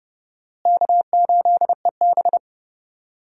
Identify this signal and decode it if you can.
Morse code, heard as a keyed tone, about 30 wpm, 700 Hz. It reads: K8EB
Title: USA Call Signs - 4 Characters - Repeated 2x - 30wpm